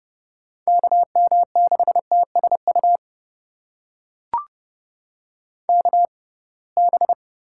Morse code, heard as a keyed tone, about 32 wpm, spaced as for 23 wpm, 700 Hz. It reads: KM6TSU KB
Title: USA Call Signs - 6 Characters - 6x Character Spacing - 30wpm